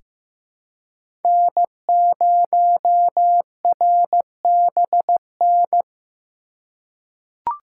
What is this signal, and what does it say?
Morse code, heard as a keyed tone, about 15 wpm, 700 Hz. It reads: N0RBN
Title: USA Call Signs - 5 Characters - 2x Character Spacing - 15wpm